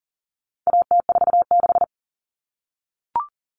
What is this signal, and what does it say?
Morse code, heard as a keyed tone, about 38 wpm, 700 Hz. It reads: AT46
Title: All Cut Numbers - 4-Digit 40wpm